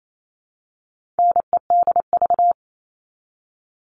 Morse code, heard as a keyed tone, about 28 wpm, 700 Hz. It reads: NEDV
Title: All Cut Numbers - 4-Digit 28wpm